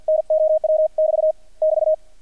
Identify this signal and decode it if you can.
Morse code, heard as a keyed tone, about 48 wpm, 630 Hz, with radio noise. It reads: NOW <BT> <BT>
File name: archived / CRSnet